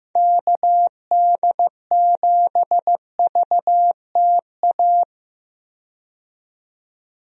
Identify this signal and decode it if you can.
Morse code, heard as a keyed tone, about 16 wpm, 700 Hz. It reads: KD7VTA
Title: USA Call Signs - 6 Characters - 15wpm